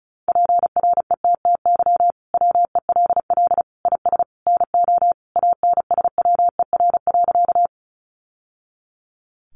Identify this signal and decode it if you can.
Morse code, heard as a keyed tone, about 34 wpm, 700 Hz. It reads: PRETTY WELL IS NO ANSWER.
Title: Sentences from Top 1000 Words 35wpm